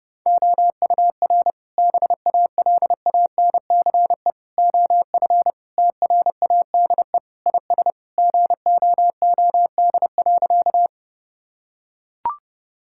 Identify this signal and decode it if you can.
Morse code, heard as a keyed tone, about 30 wpm, 700 Hz. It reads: OUR BALANCE OF TRADE IS GOOD.
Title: Sentences from Top 1000 Words - Review Part 6 - 30wpm